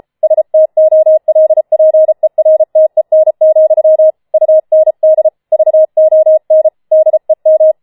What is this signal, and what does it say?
Morse code, heard as a keyed tone, about 33 wpm, 630 Hz. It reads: STOLPERTEN, UND VON DEM